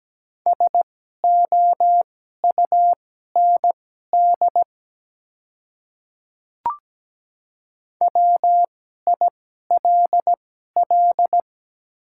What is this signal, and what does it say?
Morse code, heard as a keyed tone, about 17 wpm, spaced as for 12 wpm, 700 Hz. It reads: SOUND WILL
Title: 2x Character Spacing - Top 100 Words - 17wpm